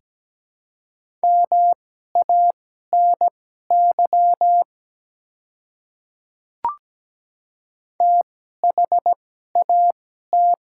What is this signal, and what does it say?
Morse code, heard as a keyed tone, about 17 wpm, spaced as for 12 wpm, 690 Hz. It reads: MANY THAT
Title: 2x Character Spacing - Top 100 Words - 17wpm